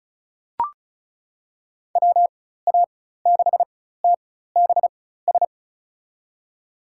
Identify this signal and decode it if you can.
Morse code, heard as a keyed tone, about 34 wpm, spaced as for 16 wpm, 700 Hz. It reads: WA6TBS
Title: USA Call Signs - 4x Character Spacing - 35wpm